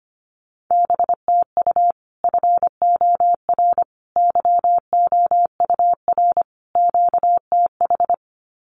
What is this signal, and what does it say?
Morse code, heard as a keyed tone, about 25 wpm, 700 Hz. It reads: BTU FOR YOUR QTH
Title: Common QSO Phrases 25wpm